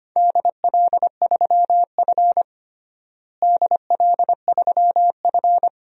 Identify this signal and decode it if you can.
Morse code, heard as a keyed tone, about 25 wpm, 700 Hz. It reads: DL3F DL3F
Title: DXCC-100 Call Signs - 4 Characters - Repeated 3x - 25wpm